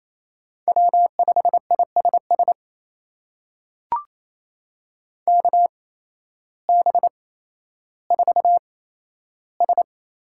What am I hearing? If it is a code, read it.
Morse code, heard as a keyed tone, about 28 wpm, spaced as for 13 wpm, 700 Hz. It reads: W5ISS KB4S
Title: USA Call Signs - 8x Character Spacing - 28wpm